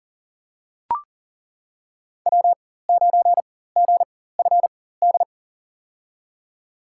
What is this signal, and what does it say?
Morse code, heard as a keyed tone, about 40 wpm, spaced as for 19 wpm, 700 Hz. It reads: W9GFD
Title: USA Call Signs - 4x Character Spacing - 40wpm